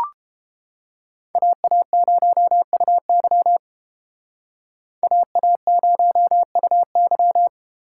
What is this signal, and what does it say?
Morse code, heard as a keyed tone, about 30 wpm, 700 Hz. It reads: AA0UY AA0UY